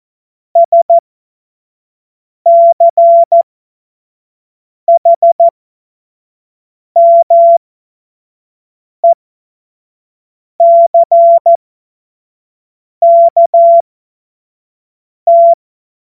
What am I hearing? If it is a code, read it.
Morse code, heard as a keyed tone, about 14 wpm, spaced as for 5 wpm, 680 Hz. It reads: SCHMECKT
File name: Morsekurs / Witze / Witz 19. Oktober 2021, 25 F70